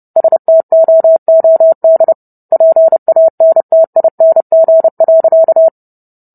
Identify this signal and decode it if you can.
Morse code, heard as a keyed tone, about 30 wpm, 650 Hz. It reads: STOOD PANTING.